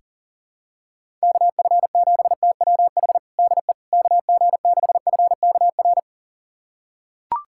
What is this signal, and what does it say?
Morse code, heard as a keyed tone, about 40 wpm, 690 Hz. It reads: KF7TWH DE KG6FKR
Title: USA Callsign DE Callsign - 6 Characters - 40wpm